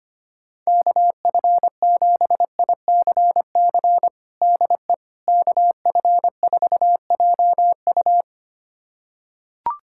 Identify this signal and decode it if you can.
Morse code, heard as a keyed tone, about 25 wpm, 700 Hz. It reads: KF7ICC DE KF4JU
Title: USA Callsign DE Callsign 25wpm